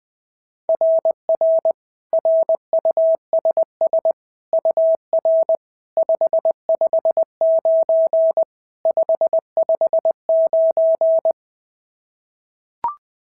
Morse code, heard as a keyed tone, about 20 wpm, 690 Hz. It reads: RR RUSS UR 559 559
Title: SOTA QSO Practice 20wpm